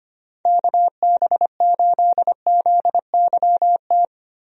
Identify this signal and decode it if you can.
Morse code, heard as a keyed tone, about 25 wpm, 700 Hz. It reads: KB8ZYT
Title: USA Call Signs - 6 Characters - Repeated 3x - 25wpm